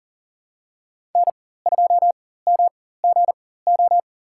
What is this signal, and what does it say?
Morse code, heard as a keyed tone, about 37 wpm, spaced as for 18 wpm, 700 Hz. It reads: N2MGO
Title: USA Call Signs - 4x Character Spacing - 40wpm